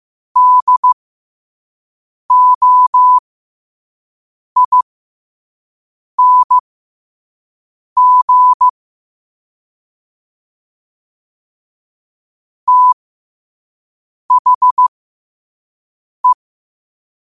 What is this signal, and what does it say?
Morse code, heard as a keyed tone, about 15 wpm, spaced as for 5 wpm, 1000 Hz. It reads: DOING THE
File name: qotdfiles / 5wpm